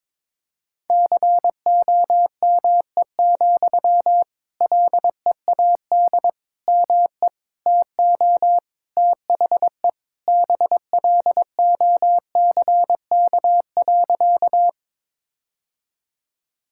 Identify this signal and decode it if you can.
Morse code, heard as a keyed tone, about 22 wpm, 700 Hz. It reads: COME, LEAD ME TO THE BLOCK.
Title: Sentences from Top 1600 Words 22wpm